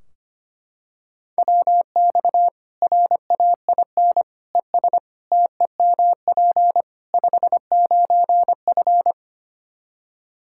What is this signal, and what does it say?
Morse code, heard as a keyed tone, about 25 wpm, 700 Hz. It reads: WX RAIN ES TEMP 59F
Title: Practice QSO - Exchange 2 - Info - 25wpm